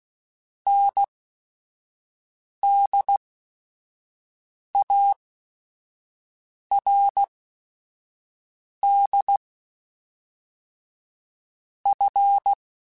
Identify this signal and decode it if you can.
Morse code, heard as a keyed tone, about 16 wpm, spaced as for 5 wpm, 800 Hz. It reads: NDARD F
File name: Technical Subjects / Technical Subjects-mp3s / AND Gate @5